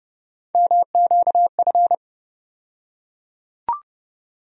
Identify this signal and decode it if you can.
Morse code, heard as a keyed tone, about 30 wpm, 700 Hz. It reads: MQF
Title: USA Call Sign Postfixes - 3 Letters 30wpm